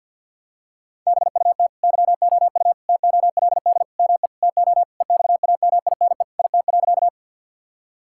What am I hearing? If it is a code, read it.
Morse code, heard as a keyed tone, about 50 wpm, 700 Hz. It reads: BUT YOU TOLD ME TO EXAMINE IT.
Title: Sentences from Top 1600 Words 50wpm